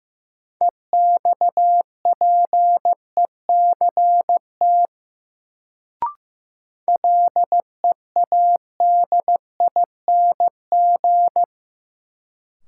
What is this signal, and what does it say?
Morse code, heard as a keyed tone, about 15 wpm, 700 Hz. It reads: EXPECT LEADING
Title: Sentences from Top 1100 Words - New Word Drill - 15wpm